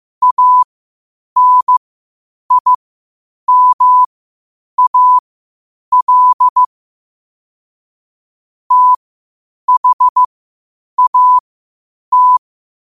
Morse code, heard as a keyed tone, about 15 wpm, spaced as for 8 wpm, 1000 Hz. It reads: ANIMAL THAT
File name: qotdfiles / 8wpm